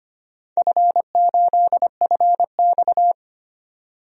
Morse code, heard as a keyed tone, about 25 wpm, 700 Hz. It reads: F8FX
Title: DXCC-100 Call Signs - 4 Characters - 25wpm